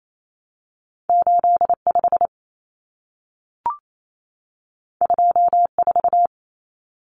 Morse code, heard as a keyed tone, about 28 wpm, 700 Hz. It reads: 85 24
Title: Common Cut Numbers - 2-Digit 28wpm